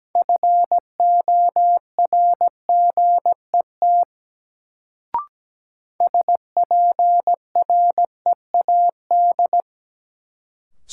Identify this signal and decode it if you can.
Morse code, heard as a keyed tone, about 17 wpm, 700 Hz. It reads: FORGET SPREAD
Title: Sentences from Top 1200 Words - New Word Drill Alt - 17wpm